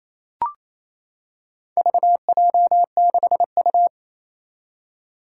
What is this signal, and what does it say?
Morse code, heard as a keyed tone, about 27 wpm, 700 Hz. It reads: VJ6U
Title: Four Challenging Letter-Number Combinations 28wpm